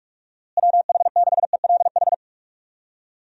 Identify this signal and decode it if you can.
Morse code, heard as a keyed tone, about 47 wpm, 700 Hz. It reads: WH6ELH
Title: USA Call Signs - 6 Characters - Repeated 2x - 45wpm